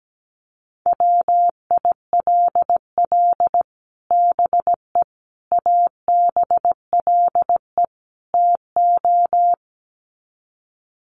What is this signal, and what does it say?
Morse code, heard as a keyed tone, about 17 wpm, 700 Hz. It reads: WILL BE ABLE TO
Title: Sets of 4 Words 17wpm